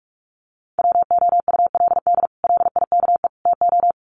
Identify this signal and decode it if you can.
Morse code, heard as a keyed tone, about 44 wpm, 700 Hz. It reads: WOULD LIKE TO
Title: Sets of 3 Words 45wpm